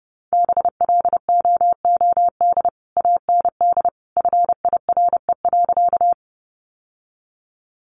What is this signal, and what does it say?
Morse code, heard as a keyed tone, about 30 wpm, 700 Hz. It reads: BLOOD AND FIRE.
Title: Sentences from Top 1100 Words 30wpm